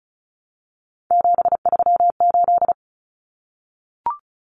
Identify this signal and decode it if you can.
Morse code, heard as a keyed tone, about 34 wpm, 700 Hz. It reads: 738
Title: Common Cut Numbers - 3-Digit 35wpm